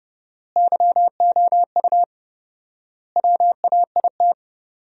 Morse code, heard as a keyed tone, about 28 wpm, 700 Hz. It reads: YOU WAIT
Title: 4x Word-Spacing - Sentences from Top 500 Words - 30wpm